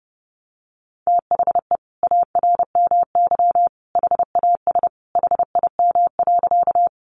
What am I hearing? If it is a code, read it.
Morse code, heard as a keyed tone, about 30 wpm, 700 Hz. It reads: THE ARMY HAS HIM.
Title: Sentences from Top 800 Words 30wpm